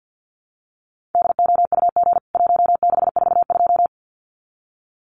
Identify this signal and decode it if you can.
Morse code, heard as a keyed tone, about 49 wpm, 700 Hz. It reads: DOUG 1642
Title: CWT Exchange 50wpm